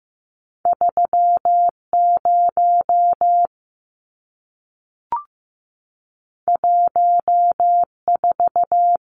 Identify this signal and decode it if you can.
Morse code, heard as a keyed tone, about 15 wpm, 700 Hz. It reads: 3 0 1 4
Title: Numbers - 2-Digit 15wpm